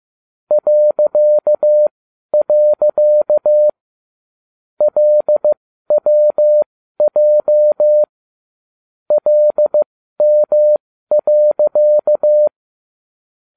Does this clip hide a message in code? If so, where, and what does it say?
Yes: Morse code from 0.5 to 12.5 s, heard as a keyed tone, about 15 wpm, spaced as for 12 wpm, 600 Hz. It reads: .. LWJ LM.